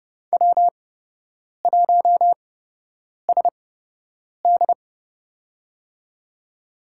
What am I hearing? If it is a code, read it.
Morse code, heard as a keyed tone, about 30 wpm, spaced as for 8 wpm, 700 Hz. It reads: W1SD
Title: USA Call Signs - 4 Characters - 8x Character Spacing - 30wpm